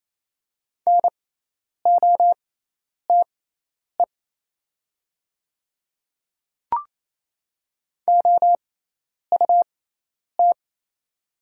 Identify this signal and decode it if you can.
Morse code, heard as a keyed tone, about 28 wpm, spaced as for 10 wpm, 700 Hz. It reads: NOTE OUT
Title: 6x Character Spacing - Top 500 Words - 28wpm